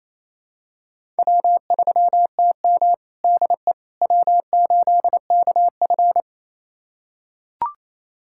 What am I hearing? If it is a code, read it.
Morse code, heard as a keyed tone, about 28 wpm, 700 Hz. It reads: W3TM DE W8KF
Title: USA Callsign DE Callsign - 4 Characters - 28wpm